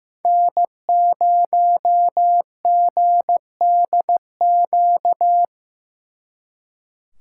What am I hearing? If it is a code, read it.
Morse code, heard as a keyed tone, about 18 wpm, 700 Hz. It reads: N0GDQ
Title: Speed Racing - USA 1x3 Call Signs 15wpm